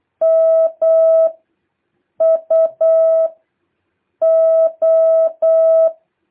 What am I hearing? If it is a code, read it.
Morse code, heard as a keyed tone, about 8 wpm, 650 Hz. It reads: M U O
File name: jh8cbh / 2023cw